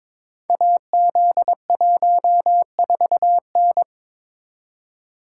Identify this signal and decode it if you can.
Morse code, heard as a keyed tone, about 22 wpm, 700 Hz. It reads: AZ14N
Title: Code Groups - 5 Letter:Numbers - 22wpm